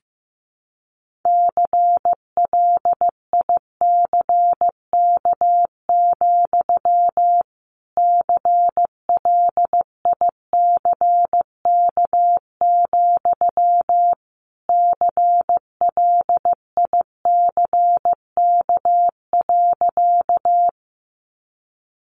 Morse code, heard as a keyed tone, about 14 wpm, 700 Hz. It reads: CLICK, CLICK, CLICK.
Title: Sentences from Top 1100 Words 15wpm